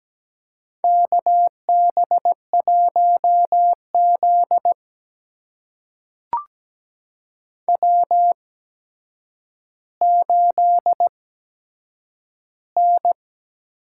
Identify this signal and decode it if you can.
Morse code, heard as a keyed tone, about 17 wpm, spaced as for 8 wpm, 700 Hz. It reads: KB1Z W8N
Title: USA Call Signs - 4 Characters - 8x Character Spacing - 17wpm